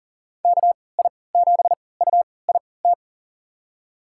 Morse code, heard as a keyed tone, about 40 wpm, spaced as for 23 wpm, 700 Hz. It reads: KI7UIT
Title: USA Call Signs - 6 Characters - 3x Character Spacing - 40wpm